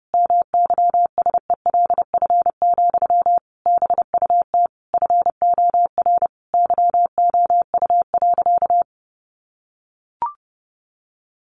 Morse code, heard as a keyed tone, about 30 wpm, 700 Hz. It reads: MYSELF, BUT FOR YOU.
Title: Sentences from Top 1300 Words 30wpm